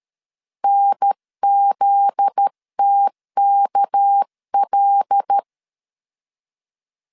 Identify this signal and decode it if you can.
Morse code, heard as a keyed tone, about 13 wpm, 800 Hz. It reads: NZTKL
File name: LEZIONI / 7 GF9Z / 7 GF9Z 12wpm 2147